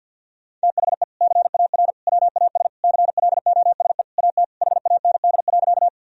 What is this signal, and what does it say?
Morse code, heard as a keyed tone, about 50 wpm, 700 Hz. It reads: THE CAR WAS CLOSE AT HAND.
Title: Sentences from Top 1000 Words - Review Part 1 - 50wpm